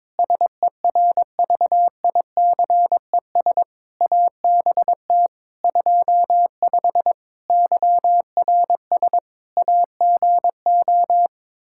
Morse code, heard as a keyed tone, about 22 wpm, 700 Hz. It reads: SERVICES ABT 25 YRS AGO
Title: Practice QSO - Exchange 3 - More Info - 22wpm